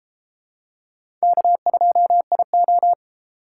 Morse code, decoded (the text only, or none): K2IO